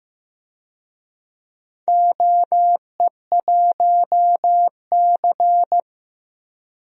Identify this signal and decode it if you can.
Morse code, heard as a keyed tone, about 16 wpm, 700 Hz. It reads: OE1C
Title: DXCC-100 Call Signs - 4 Characters - Repeated 3x - 15wpm